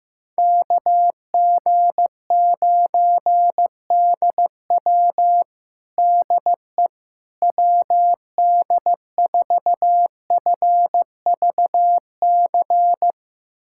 Morse code, heard as a keyed tone, about 16 wpm, 700 Hz. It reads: KG9DW DE WD4FVC
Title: USA Callsign DE Callsign 15wpm